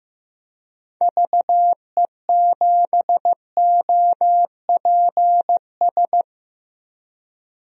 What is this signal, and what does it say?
Morse code, heard as a keyed tone, about 15 wpm, 700 Hz. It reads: VE7OPS
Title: DXCC-100 Call Signs - 6 Characters - 15wpm